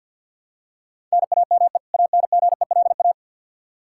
Morse code, heard as a keyed tone, about 48 wpm, 700 Hz. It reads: NAME ANGELA
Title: K1USN SST QSO Practice 50wpm